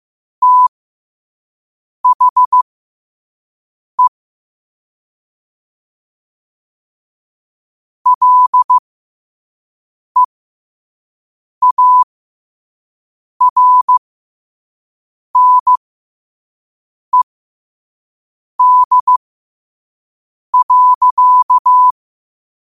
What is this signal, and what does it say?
Morse code, heard as a keyed tone, about 15 wpm, spaced as for 5 wpm, 1000 Hz. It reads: THE LEARNED.
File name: qotdfiles / 5wpm